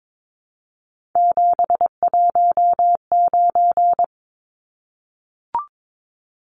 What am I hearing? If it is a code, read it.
Morse code, heard as a keyed tone, about 22 wpm, 700 Hz. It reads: 719
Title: Numbers - 3-Digit 22wpm